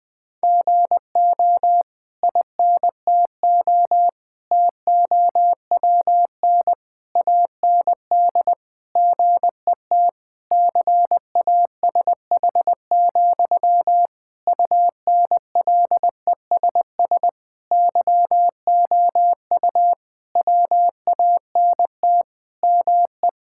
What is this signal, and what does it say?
Morse code, heard as a keyed tone, about 19 wpm, 700 Hz. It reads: GO INTO TOWN AND GET CASH, UNLESS YOU WANT ME
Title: Sentences from Top 1000 Words - Review Part 6 - 20wpm